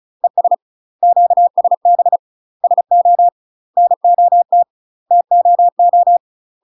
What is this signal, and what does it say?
Morse code, heard as a keyed tone, about 34 wpm, 700 Hz. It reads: ES QSB SO NOT TOO
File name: UK MASTER QSOs TRAINING / 35WPM / UK MASTER QSOs 0001